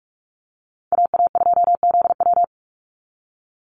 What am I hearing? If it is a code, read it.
Morse code, heard as a keyed tone, about 47 wpm, 700 Hz. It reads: AA2ZW
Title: Call Signs Repeated 2x - 45wpm